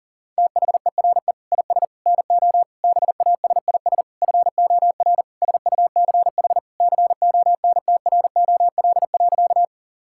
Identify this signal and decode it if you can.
Morse code, heard as a keyed tone, about 37 wpm, 700 Hz. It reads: THERE IS NO BASIS FOR SUCH CONTROL.
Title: Sentences from Top 1000 Words - Review Part 5 - 40wpm